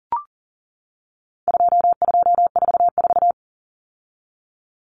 Morse code, heard as a keyed tone, about 40 wpm, 700 Hz. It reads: 2244
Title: Common Cut Numbers - 4-Digit 40wpm